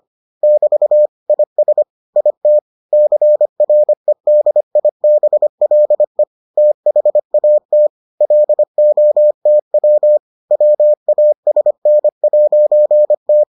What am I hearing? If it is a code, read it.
Morse code, heard as a keyed tone, about 25 wpm, 600 Hz. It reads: <BT> IS IT CREDIBLE THAT LOTW WASN'T